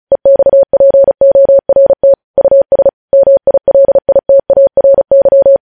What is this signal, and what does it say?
Morse code, heard as a keyed tone, about 36 wpm, 550 Hz. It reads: EXPORT US MILITARY
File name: podcast / News headlines-1x-35x35